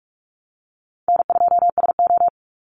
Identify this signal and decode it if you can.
Morse code, heard as a keyed tone, about 44 wpm, 700 Hz. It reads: N2SO
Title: Easy Call Signs Repeated 2x - 45wpm